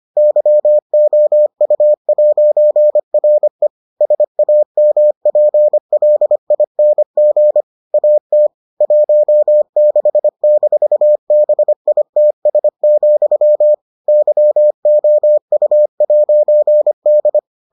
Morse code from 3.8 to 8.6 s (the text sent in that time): SAMPLING AT